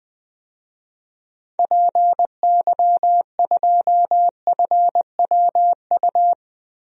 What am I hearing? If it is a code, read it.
Morse code, heard as a keyed tone, about 20 wpm, 700 Hz. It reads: PY2FWU
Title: DXCC-100 Call Signs - 6 Characters - Repeated 2x - 20wpm